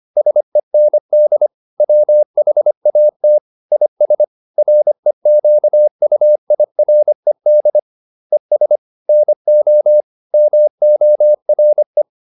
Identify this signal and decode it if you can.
Morse code, heard as a keyed tone, about 25 wpm, 600 Hz. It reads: SEND WHAT IS REQUIRED ES NO MORE